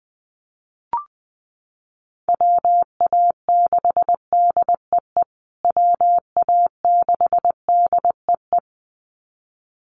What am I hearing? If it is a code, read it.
Morse code, heard as a keyed tone, about 20 wpm, 700 Hz. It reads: WA6DEE WA6DEE
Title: USA Call Signs Repeated 2x 20wpm